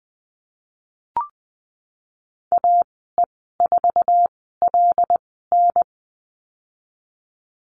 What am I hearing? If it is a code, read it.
Morse code, heard as a keyed tone, about 20 wpm, spaced as for 15 wpm, 700 Hz. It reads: AE4LN
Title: 088 Lesson 15g - Number 5 - Call Signs 20wpm